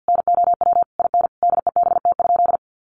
Morse code, heard as a keyed tone, about 50 wpm, 700 Hz. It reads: NOW IN DEBT?
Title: Sentences from Top 1300 Words 50wpm